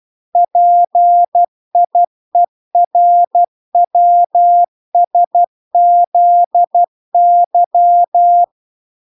Morse code, heard as a keyed tone, about 12 wpm, 700 Hz. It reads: PIERWSZY